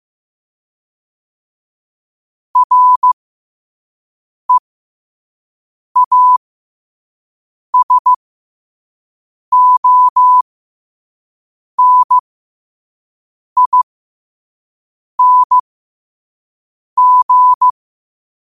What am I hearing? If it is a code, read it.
Morse code, heard as a keyed tone, about 15 wpm, spaced as for 5 wpm, 1000 Hz. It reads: REASONING
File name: qotdfiles / 5wpm